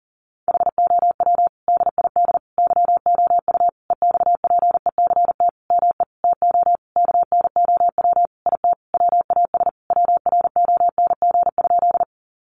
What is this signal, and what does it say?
Morse code, heard as a keyed tone, about 40 wpm, 700 Hz. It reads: HOW DID YOU EXPECT ME TO KNOW IT WAS WRONG?
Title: Sentences from Top 1100 Words 40wpm